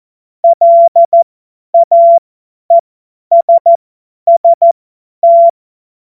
Morse code, heard as a keyed tone, about 14 wpm, spaced as for 10 wpm, 680 Hz. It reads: LAESST
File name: Morsekurs / Witze / Witz 21. Juni 2022 50 F70